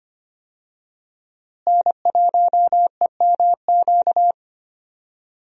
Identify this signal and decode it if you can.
Morse code, heard as a keyed tone, about 25 wpm, 700 Hz. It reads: N1EMQ
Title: USA Call Signs - 5 Characters - Repeated 2x - 25wpm